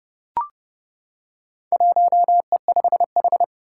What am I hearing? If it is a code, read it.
Morse code, heard as a keyed tone, about 30 wpm, 700 Hz. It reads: 1E5H
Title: Four Challenging Letter-Number Combinations 30wpm